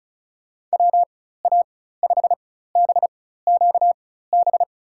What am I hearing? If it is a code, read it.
Morse code, heard as a keyed tone, about 36 wpm, spaced as for 16 wpm, 700 Hz. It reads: WA5BQB
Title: USA Call Signs - 6 Characters - 4x Character Spacing - 35wpm